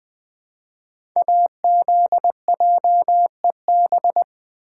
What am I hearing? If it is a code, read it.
Morse code, heard as a keyed tone, about 20 wpm, 700 Hz. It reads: AZJEB
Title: Code Groups - 5 Letter:Numbers - 20wpm